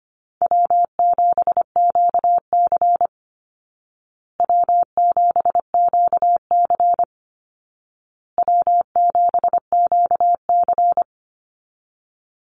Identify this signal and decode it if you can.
Morse code, heard as a keyed tone, about 25 wpm, 700 Hz. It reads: W7QC W7QC W7QC
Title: Easy Call Signs Repeated 3x - 25wpm